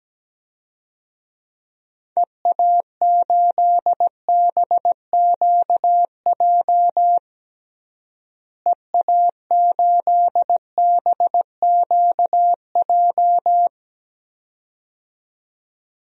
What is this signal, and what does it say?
Morse code, heard as a keyed tone, about 17 wpm, 700 Hz. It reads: EA8BQJ EA8BQJ
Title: DXCC-100 Call Signs - 6 Characters - Repeated 2x - 17wpm